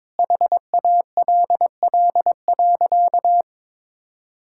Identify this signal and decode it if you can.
Morse code, heard as a keyed tone, about 22 wpm, 700 Hz. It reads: HALL.
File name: practice / 22wpm / Sentences with 8-Word Phrases - 22wpm